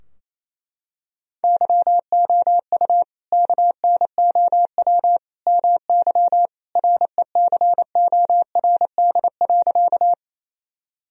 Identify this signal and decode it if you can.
Morse code, heard as a keyed tone, about 28 wpm, 700 Hz. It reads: YOU KNOW MY RECORD.
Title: Sentences from Top 1000 Words - Review Part 3 - 28wpm